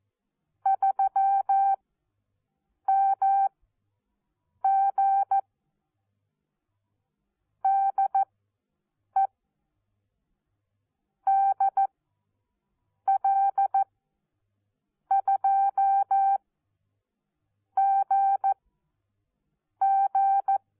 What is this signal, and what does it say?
Morse code, heard as a keyed tone, about 14 wpm, spaced as for 6 wpm, 780 Hz. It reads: 3MG DE DL2GG
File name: Onlinelehrgang / morsen / DJ4UF Morsen14